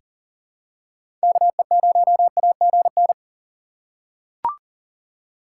Morse code, heard as a keyed tone, about 40 wpm, 700 Hz. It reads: KE0AGN
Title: USA Call Signs - 6 Characters - 6x Character Spacing - 40wpm